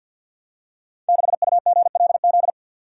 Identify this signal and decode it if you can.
Morse code, heard as a keyed tone, about 50 wpm, 700 Hz. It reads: 6UGLZ
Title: Code Groups - 5 Letter:Numbers - 50wpm